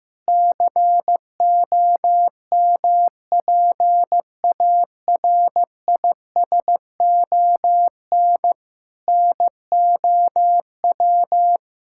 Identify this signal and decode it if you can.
Morse code, heard as a keyed tone, about 16 wpm, 700 Hz. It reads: COMPARISON NOW
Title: Sentences from Top 1600 Words 15wpm